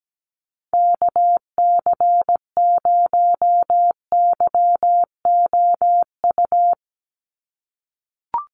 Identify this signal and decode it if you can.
Morse code, heard as a keyed tone, about 17 wpm, 700 Hz. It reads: KC0YOU
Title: USA Call Signs 17wpm